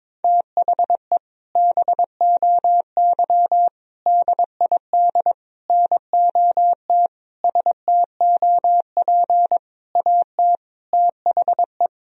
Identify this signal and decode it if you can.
Morse code, heard as a keyed tone, about 22 wpm, 700 Hz. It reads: THE BOY DID NOT STOP AT THE
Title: Sentences from Top 1000 Words - Review Part 2 - 22wpm